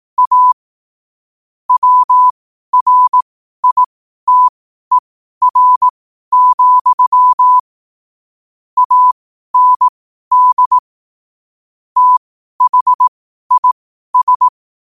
Morse code, heard as a keyed tone, about 18 wpm, spaced as for 12 wpm, 1000 Hz. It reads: A WRITER, AND THIS
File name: qotdfiles / 12wpm